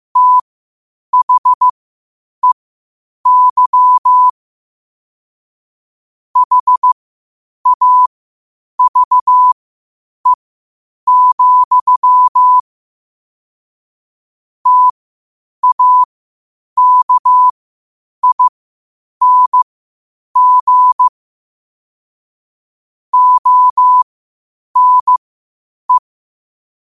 Morse code, heard as a keyed tone, about 15 wpm, spaced as for 8 wpm, 1000 Hz. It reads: THEY HAVE, TAKING ONE
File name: 8wpm